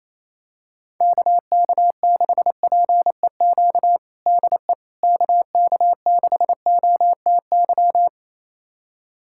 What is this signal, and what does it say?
Morse code, heard as a keyed tone, about 28 wpm, 700 Hz. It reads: KK6PEQ DE KK6OTY
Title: USA Callsign DE Callsign - 6 Characters - 28wpm